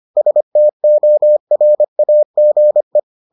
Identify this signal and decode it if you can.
Morse code, heard as a keyed tone, about 25 wpm, 600 Hz. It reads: STORAGE